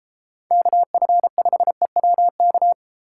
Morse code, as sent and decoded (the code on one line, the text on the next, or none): -.- ..-. ..... .-- -.-
KF5EWK